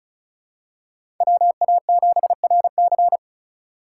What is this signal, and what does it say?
Morse code, heard as a keyed tone, about 36 wpm, 700 Hz. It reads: WA7RC
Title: USA Call Signs - 2x Character Spacing - 35wpm